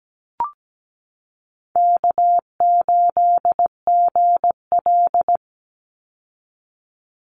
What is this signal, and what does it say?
Morse code, heard as a keyed tone, about 17 wpm, 700 Hz. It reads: K8GL
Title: Easy Call Signs Repeated 3x - 17wpm